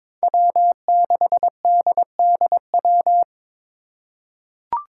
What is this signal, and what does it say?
Morse code, heard as a keyed tone, about 22 wpm, 700 Hz. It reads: W6DDW
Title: Five Challenging Letter-Number Combinations 22wpm